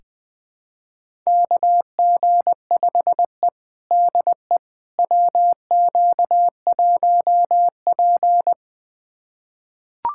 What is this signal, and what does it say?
Morse code, heard as a keyed tone, about 20 wpm, 700 Hz. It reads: KG5E DE WQ1P
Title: USA Callsign DE Callsign - 4 Characters - 20wpm